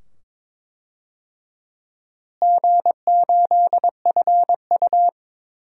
Morse code, heard as a keyed tone, about 22 wpm, 700 Hz. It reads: G8FU
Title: DXCC-100 Call Signs - 4 Characters - Repeated 2x - 22wpm